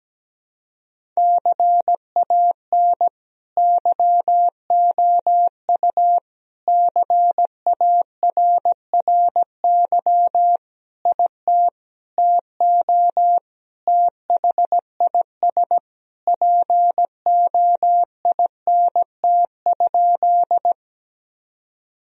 Morse code, heard as a keyed tone, about 17 wpm, 700 Hz. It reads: CAN YOU CARRY IT TO THIS POINT?
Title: Sentences from Top 1000 Words - Review Part 1 - 17wpm